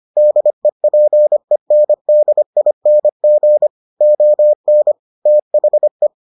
Morse code, heard as a keyed tone, about 25 wpm, 600 Hz. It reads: DEPENDING ON THE